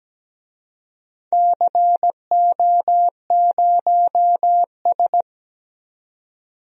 Morse code, heard as a keyed tone, about 17 wpm, 700 Hz. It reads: CO0S
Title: DXCC-100 Call Signs - 4 Characters - 17wpm